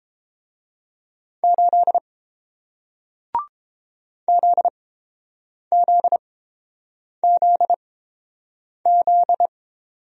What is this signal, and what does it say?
Morse code, heard as a keyed tone, about 30 wpm, spaced as for 7 wpm, 700 Hz. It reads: 8 ZZZZ